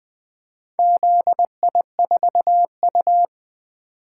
Morse code, heard as a keyed tone, about 21 wpm, 700 Hz. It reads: ZI4U
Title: Four Letter-Number Combinations - 20wpm